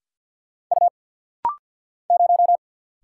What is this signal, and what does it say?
Morse code, heard as a keyed tone, about 50 wpm, 700 Hz. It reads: U 0
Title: Single Letter-Number Alt - Rapid-Fire 50wpm